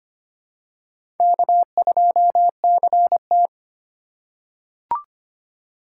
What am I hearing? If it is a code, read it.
Morse code, heard as a keyed tone, about 25 wpm, 700 Hz. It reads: K2CT
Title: USA Call Signs - 4 Characters - 3x Character Spacing - 25wpm